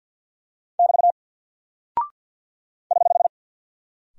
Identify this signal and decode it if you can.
Morse code, heard as a keyed tone, about 50 wpm, 700 Hz. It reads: <BT> <HH>
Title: Single Character (Letter:Number:Punctuation:Prosign) 50wpm